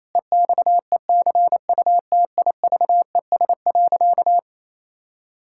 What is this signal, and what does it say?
Morse code, heard as a keyed tone, about 28 wpm, 700 Hz. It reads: EXECUTIVES.
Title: Pangram Challenge v2 - 28wpm